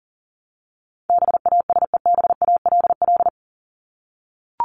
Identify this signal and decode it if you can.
Morse code, heard as a keyed tone, about 40 wpm, 700 Hz. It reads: BASEBALL
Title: Top 1751-2000 Words 40wpm